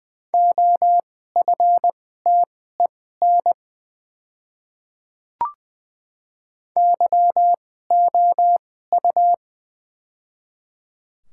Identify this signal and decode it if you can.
Morse code, heard as a keyed tone, about 20 wpm, spaced as for 15 wpm, 700 Hz. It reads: OFTEN YOU
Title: 2x Character Spacing - Top 300 Words - 20wpm